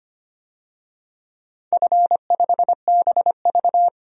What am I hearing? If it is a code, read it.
Morse code, heard as a keyed tone, about 25 wpm, 700 Hz. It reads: F5BV
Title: DXCC-100 Call Signs - 4 Characters - 25wpm